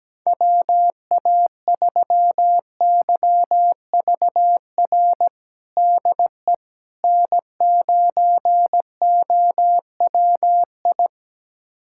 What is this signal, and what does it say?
Morse code, heard as a keyed tone, about 17 wpm, 700 Hz. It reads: WA3YVR DE N9OWI
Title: USA Callsign DE Callsign 17wpm